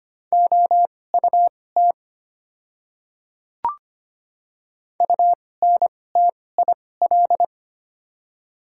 Morse code, heard as a keyed tone, about 25 wpm, spaced as for 18 wpm, 700 Hz. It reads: OUT UNTIL
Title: 2x Character Spacing - Top 300 Words - 25wpm